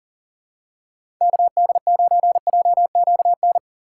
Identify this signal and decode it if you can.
Morse code, heard as a keyed tone, about 37 wpm, 700 Hz. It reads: KD9JQN
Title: USA Call Signs - 6 Characters - 4x Character Spacing - 40wpm